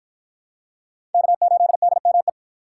Morse code, heard as a keyed tone, about 51 wpm, 700 Hz. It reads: K8DME